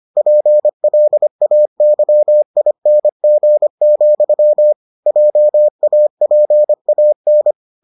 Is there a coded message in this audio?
Yes: Morse code from 0.2 to 7.5 s, heard as a keyed tone, about 25 wpm, 600 Hz. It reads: PLAYING, JAPAN